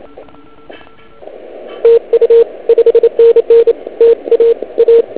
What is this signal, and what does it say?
Morse code, heard as a keyed tone, about 28 wpm, 460 Hz. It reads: TU 5NN TA A